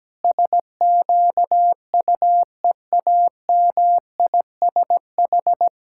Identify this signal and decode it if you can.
Morse code, heard as a keyed tone, about 17 wpm, 700 Hz. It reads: SQUEAMISH
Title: Pangram Challenge v2 - 17wpm